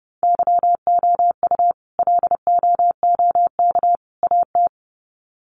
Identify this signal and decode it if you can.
Morse code, heard as a keyed tone, about 30 wpm, 700 Hz. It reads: YOU LOOK AT
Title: Sets of 3 Words 30wpm